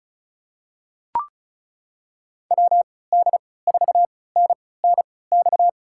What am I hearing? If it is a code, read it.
Morse code, heard as a keyed tone, about 36 wpm, spaced as for 20 wpm, 700 Hz. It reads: WD4NNX
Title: USA Call Signs - 3x Character Spacing - 35wpm